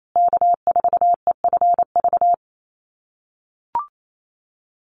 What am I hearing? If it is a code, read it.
Morse code, heard as a keyed tone, about 28 wpm, 700 Hz. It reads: K4EFV
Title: USA Call Signs Repeated 3x 28wpm